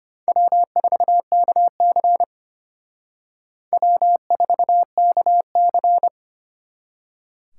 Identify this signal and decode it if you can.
Morse code, heard as a keyed tone, about 32 wpm, 700 Hz. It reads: W4KC W4KC